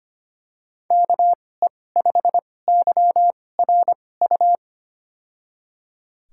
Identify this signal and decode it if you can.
Morse code, heard as a keyed tone, about 25 wpm, spaced as for 18 wpm, 700 Hz. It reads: KE5YRU
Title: USA Call Signs - 6 Characters - 2x Character Spacing - 25wpm